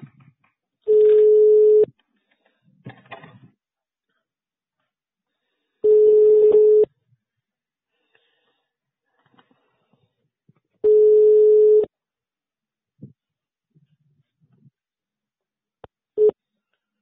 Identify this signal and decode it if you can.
Ringback tone (Europe, 425 Hz).
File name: sauvegarde / ACRRecordings- -jusquau 17-juin-2017 / Mme Jenifer. Bloc4 Joseph Durbuy 2017 05 27 21 41 04